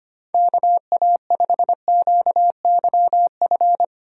Morse code, heard as a keyed tone, about 25 wpm, 700 Hz. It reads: KA5QYF
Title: USA Call Signs - 6 Characters - 25wpm